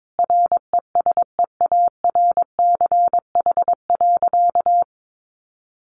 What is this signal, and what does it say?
Morse code, heard as a keyed tone, about 22 wpm, 700 Hz. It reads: RESEARCH.
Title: Sentences from Top 1400 Words 22wpm